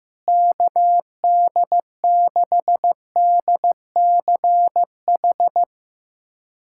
Morse code, heard as a keyed tone, about 15 wpm, 700 Hz. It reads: KD6DCH
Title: USA Callsign DE Callsign - 6 Characters - 15wpm